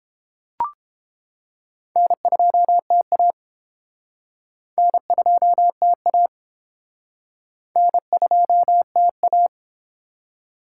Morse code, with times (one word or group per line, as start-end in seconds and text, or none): 2.0-3.3 N2TA
4.8-6.3 N2TA
7.8-9.5 N2TA